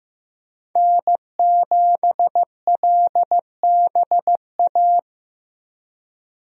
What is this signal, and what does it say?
Morse code, heard as a keyed tone, about 14 wpm, 700 Hz. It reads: N7LBA
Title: USA Call Signs - 5 Characters - Repeated 2x - 15wpm